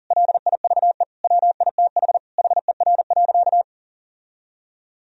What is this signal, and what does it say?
Morse code, heard as a keyed tone, about 40 wpm, 700 Hz. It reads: LIVE WITH HER.
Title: Sentences from Top 1000 Words - Review Part 3 - 40wpm